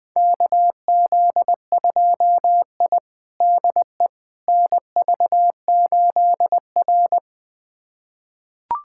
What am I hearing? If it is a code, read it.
Morse code, heard as a keyed tone, about 20 wpm, 700 Hz. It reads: KZ2I DE NV8R
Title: USA Callsign DE Callsign - 4 Characters - 20wpm